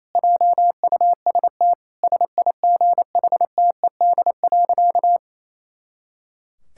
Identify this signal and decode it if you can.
Morse code, heard as a keyed tone, about 28 wpm, 700 Hz. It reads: JUST SIGHTED.
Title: Pangram Challenge v2 - 28wpm